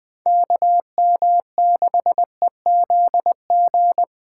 Morse code, heard as a keyed tone, about 20 wpm, 700 Hz. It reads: KM6EZG